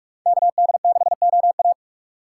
Morse code, heard as a keyed tone, about 41 wpm, 700 Hz. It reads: KD6OA